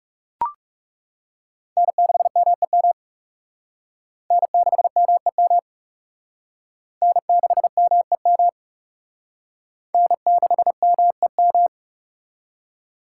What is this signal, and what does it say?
Morse code, heard as a keyed tone, about 40 wpm, 700 Hz. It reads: N6MEM N6MEM N6MEM N6MEM